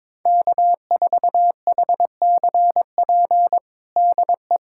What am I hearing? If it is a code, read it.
Morse code, heard as a keyed tone, about 22 wpm, 700 Hz. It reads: K4HCP DE
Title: USA Callsign DE Callsign - 5 Characters - 22wpm